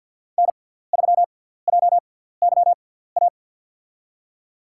Morse code, heard as a keyed tone, about 53 wpm, spaced as for 17 wpm, 700 Hz. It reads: N3JYA
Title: USA Call Signs - 5 Characters - 6x Character Spacing - 50wpm